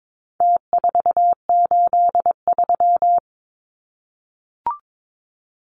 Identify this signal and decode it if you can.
Morse code, heard as a keyed tone, about 22 wpm, 700 Hz. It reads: T483
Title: Numbers - Contest Style - 4-Digit 22wpm